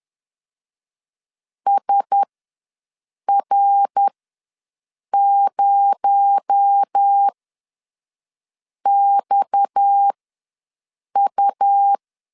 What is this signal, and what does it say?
Morse code, heard as a keyed tone, about 10 wpm, spaced as for 5 wpm, 800 Hz. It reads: SR0XU
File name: LEZIONI / 9 2 ws 10wpm 4130